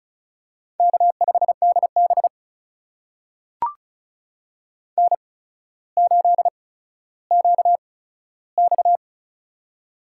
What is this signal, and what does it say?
Morse code, heard as a keyed tone, about 36 wpm, spaced as for 15 wpm, 700 Hz. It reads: K5DB N8QX
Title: USA Call Signs - 4 Characters - 8x Character Spacing - 35wpm